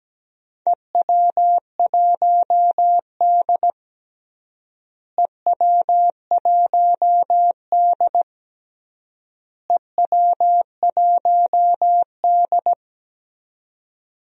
Morse code, heard as a keyed tone, about 17 wpm, 700 Hz. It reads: EW1D EW1D EW1D
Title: DXCC-100 Call Signs - 4 Characters - Repeated 3x - 17wpm